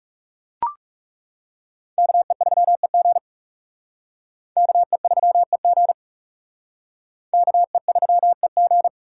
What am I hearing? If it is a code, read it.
Morse code, heard as a keyed tone, about 36 wpm, 700 Hz. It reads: KE3EG KE3EG KE3EG
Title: Speed Racing - USA 2x2 Call Signs 30wpm